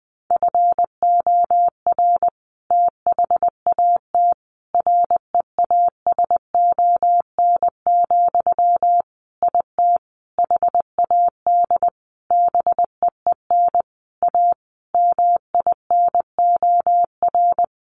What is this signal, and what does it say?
Morse code, heard as a keyed tone, about 20 wpm, 700 Hz. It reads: FOR THAT REASON, IT HAD BEEN A MINOR
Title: Sentences from Top 1500 Words 20wpm